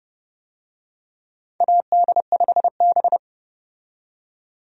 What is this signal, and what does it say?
Morse code, heard as a keyed tone, about 30 wpm, 700 Hz. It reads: AD5B
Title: USA Call Signs - 4 Characters - 30wpm